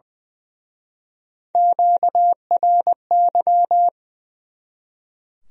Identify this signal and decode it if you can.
Morse code, heard as a keyed tone, about 24 wpm, 700 Hz. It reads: QRY
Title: Speed-Racing Q-Codes - 20wpm